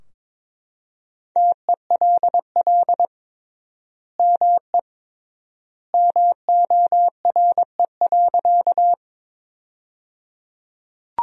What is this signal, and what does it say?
Morse code, heard as a keyed tone, about 21 wpm, 700 Hz. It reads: TELL ME MORE.
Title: 3x Word-Spacing - Sentences from Top 300 Words - 22wpm